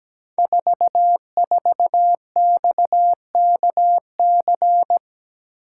Morse code, heard as a keyed tone, about 17 wpm, 700 Hz. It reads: 44XKC
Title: Code Groups - 5 Letter:Numbers - 17wpm